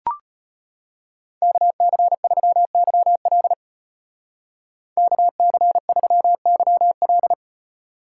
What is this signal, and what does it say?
Morse code, heard as a keyed tone, about 35 wpm, 700 Hz. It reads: KC3YL KC3YL